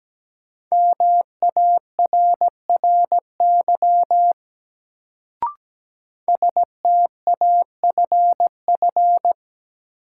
Morse code, heard as a keyed tone, about 17 wpm, 700 Hz. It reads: MARRY STAFF
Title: Sentences from Top 1200 Words - New Word Drill Alt - 17wpm